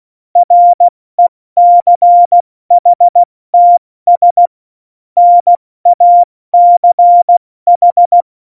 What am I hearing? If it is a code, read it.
Morse code, heard as a keyed tone, about 16 wpm, spaced as for 14 wpm, 700 Hz. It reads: RECHTS NACH
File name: Morsekurs / Witze / Witz 25. Dezember, 70 F80